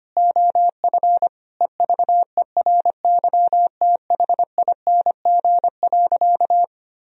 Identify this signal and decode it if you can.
Morse code, heard as a keyed tone, about 25 wpm, 700 Hz. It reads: OF EVERYTHING.
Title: Sentences from Top 1000 Words - Review Part 2 - 25wpm